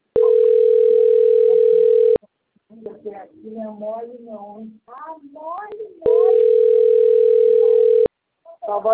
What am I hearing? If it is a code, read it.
Ringback tone (North America), 440 + 480 Hz, with speech.
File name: rg-1600-unknown- - - .33802